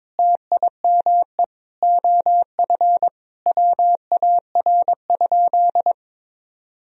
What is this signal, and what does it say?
Morse code, heard as a keyed tone, about 22 wpm, 700 Hz. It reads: TIME OF WAR?
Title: Sentences from Top 1000 Words - Review Part 2 - 22wpm